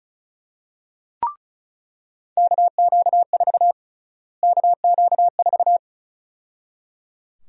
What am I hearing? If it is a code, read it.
Morse code, heard as a keyed tone, about 34 wpm, 700 Hz. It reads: KQ4 KQ4
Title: USA Call Sign Prefixes - Repeated 2x - 35wpm